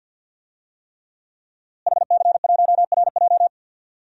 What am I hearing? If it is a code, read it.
Morse code, heard as a keyed tone, about 47 wpm, 700 Hz. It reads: HC1RJ
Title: DXCC-100 Call Signs - 5 Characters - Repeated 3x - 50wpm